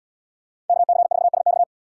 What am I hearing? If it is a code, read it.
Morse code, heard as a keyed tone, about 77 wpm, 700 Hz. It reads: KC4SX